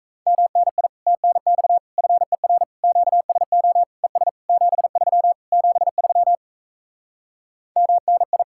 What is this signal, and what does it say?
Morse code, heard as a keyed tone, about 38 wpm, 700 Hz. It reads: MNI TNX FER QSO ES 73 73 MNI